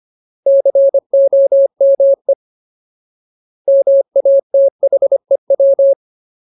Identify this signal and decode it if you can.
Morse code, heard as a keyed tone, about 25 wpm, 550 Hz. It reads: COME MATHEW
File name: Ultimate Word List / 25WPM Ultimate